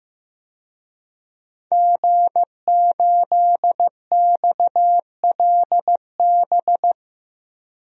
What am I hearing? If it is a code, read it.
Morse code, heard as a keyed tone, about 15 wpm, 700 Hz. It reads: G8XLB
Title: DXCC-100 Call Signs - 5 Characters - Repeated 3x - 15wpm